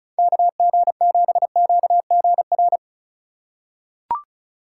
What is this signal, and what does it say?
Morse code, heard as a keyed tone, about 36 wpm, 700 Hz. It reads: KG7QGR
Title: USA Call Signs - 4x Character Spacing - 35wpm